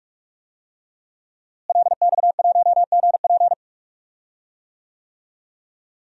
Morse code, heard as a keyed tone, about 46 wpm, 700 Hz. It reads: LX1GP
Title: DXCC-100 Call Signs - 5 Characters - Repeated 2x - 45wpm